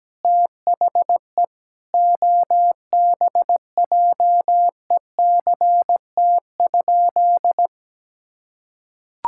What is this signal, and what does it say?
Morse code, heard as a keyed tone, about 17 wpm, 700 Hz. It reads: THE OBJECT?
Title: Sentences from Top 1000 Words - Review Part 5 - 17wpm